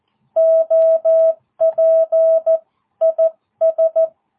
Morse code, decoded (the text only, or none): OP IS